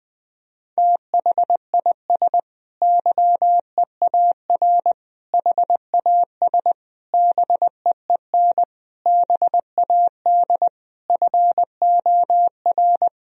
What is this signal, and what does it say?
Morse code, heard as a keyed tone, about 21 wpm, 700 Hz. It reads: THIS YEAR HAS BEEN BAD FOR
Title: Sentences from Top 1000 Words - Review Part 3 - 20wpm